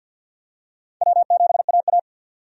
Morse code, heard as a keyed tone, about 53 wpm, 700 Hz. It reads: W7AA
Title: USA Call Signs - 4 Characters - 6x Character Spacing - 50wpm